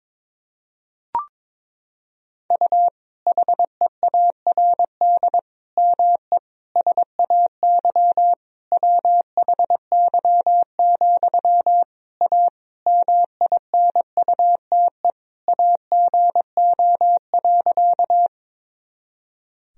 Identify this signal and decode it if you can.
Morse code, heard as a keyed tone, about 22 wpm, 700 Hz. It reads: U HEARD ME SAY WHY, A MINUTE AGO.
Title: Sentences from Top 500 Words and Abbreviations 22wpm